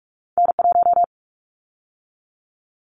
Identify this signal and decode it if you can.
Morse code, heard as a keyed tone, about 46 wpm, 700 Hz. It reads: N1
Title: Callsign Prefixes for Top 25 DXCC - 45wpm